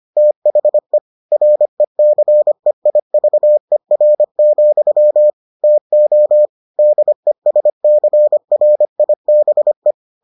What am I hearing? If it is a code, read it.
Morse code, heard as a keyed tone, about 25 wpm, 600 Hz. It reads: THE RECEIVER, TO DESCRIBE